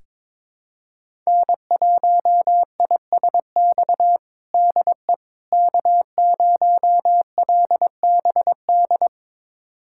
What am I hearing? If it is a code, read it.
Morse code, heard as a keyed tone, about 22 wpm, 700 Hz. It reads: N1ISX DE K0LBD
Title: USA Callsign DE Callsign - 5 Characters - 22wpm